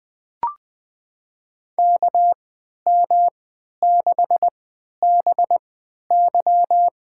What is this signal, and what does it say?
Morse code, heard as a keyed tone, about 20 wpm, spaced as for 11 wpm, 700 Hz. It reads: KM6BY
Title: USA Call Signs - 3x Character Spacing - 20wpm